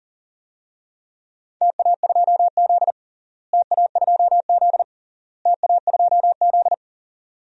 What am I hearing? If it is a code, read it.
Morse code, heard as a keyed tone, about 37 wpm, 700 Hz. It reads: TA2Z TA2Z TA2Z
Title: DXCC-100 Call Signs - 4 Characters - Repeated 3x - 40wpm